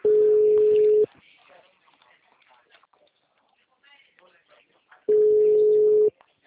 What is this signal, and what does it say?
Ringback tone (Europe, 425 Hz), with speech.